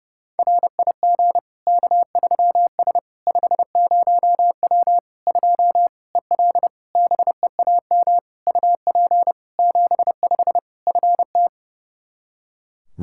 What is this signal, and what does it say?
Morse code, heard as a keyed tone, about 30 wpm, 700 Hz. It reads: RIG K3S 50W 2 EL BEAM UP 75 FT
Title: Practice QSO - Exchange 2 - Info - 30wpm